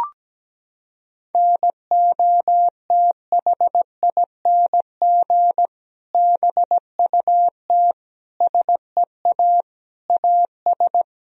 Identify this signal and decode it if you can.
Morse code, heard as a keyed tone, about 17 wpm, 700 Hz. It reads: NOTHING BUT SEA AS
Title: Sentences from Top 1000 Words - Review Part 1 - 17wpm